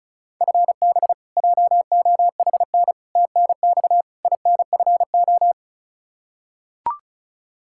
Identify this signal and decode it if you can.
Morse code, heard as a keyed tone, about 34 wpm, 700 Hz. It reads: FB JOHN TNX INFO
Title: Practice QSO - Exchange 3 - More Info - 35wpm